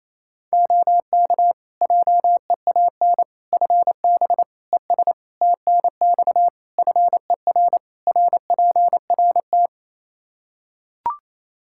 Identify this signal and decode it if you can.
Morse code, heard as a keyed tone, about 28 wpm, 700 Hz. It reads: OK JEAN FB ES TNX FER RPRT
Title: Practice QSO - Exchange 2 - Info - 28wpm